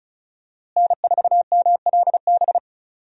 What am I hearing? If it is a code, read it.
Morse code, heard as a keyed tone, about 36 wpm, 700 Hz. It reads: N4MLB
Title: USA Call Signs - 5 Characters - Repeated 3x - 35wpm